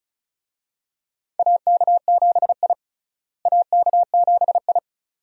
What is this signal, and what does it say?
Morse code, heard as a keyed tone, about 36 wpm, 700 Hz. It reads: AK7I AK7I
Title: USA Call Signs - 4 Characters - Repeated 2x - 35wpm